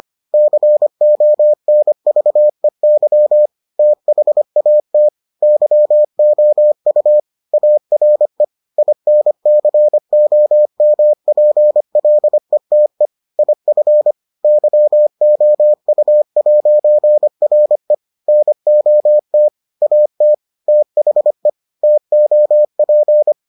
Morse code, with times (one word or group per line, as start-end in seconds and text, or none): 0.3-3.5 CONVEY
3.8-5.1 THAT
5.4-7.2 YOU
7.5-8.4 ARE
8.8-13.1 INCOMPLETE
13.4-14.1 IF
14.4-17.9 YOU'RE
18.3-19.5 NOT
19.8-20.3 AT
20.7-21.5 THE
21.8-23.3 TOP